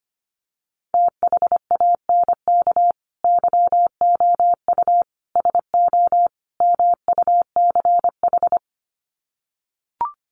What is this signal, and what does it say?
Morse code, heard as a keyed tone, about 25 wpm, 700 Hz. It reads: THANK YOU SO MUCH
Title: Sets of 4 Words 25wpm